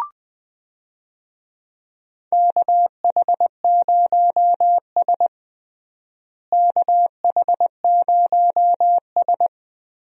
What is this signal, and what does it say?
Morse code, heard as a keyed tone, about 20 wpm, 700 Hz. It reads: KH0S KH0S
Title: USA Call Signs - 4 Characters - Repeated 3x - 20wpm